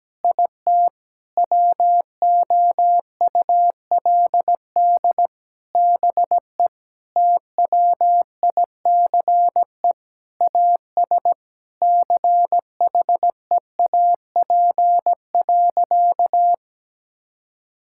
Morse code, heard as a keyed tone, about 17 wpm, 700 Hz. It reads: IT WOULD BE TWICE AS CHEAP.
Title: Sentences from Top 1600 Words 17wpm